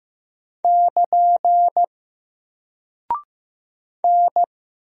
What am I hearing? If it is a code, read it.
Morse code, heard as a keyed tone, about 15 wpm, 700 Hz. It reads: <KN> N
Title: Single Character (Letter:Number:Punctuation:Prosign) 15wpm